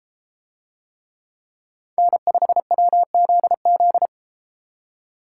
Morse code, heard as a keyed tone, about 32 wpm, 700 Hz. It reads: N5WZZ